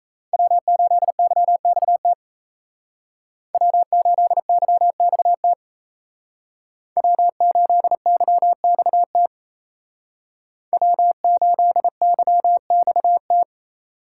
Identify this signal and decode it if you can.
Morse code, heard as a keyed tone, about 36 wpm, 700 Hz. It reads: W8YXT W8YXT W8YXT W8YXT